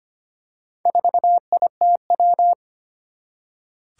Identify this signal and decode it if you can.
Morse code, heard as a keyed tone, about 25 wpm, 700 Hz. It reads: 4ITW
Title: Four Challenging Letter-Number Combinations 25wpm